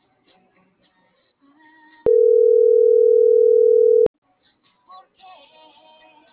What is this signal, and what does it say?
Ringback tone (North America), 440 + 480 Hz.